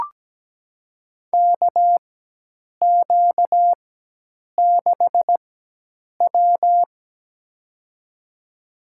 Morse code, heard as a keyed tone, about 17 wpm, spaced as for 8 wpm, 700 Hz. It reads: KQ6W
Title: USA Call Signs - 4 Characters - 4x Character Spacing - 17wpm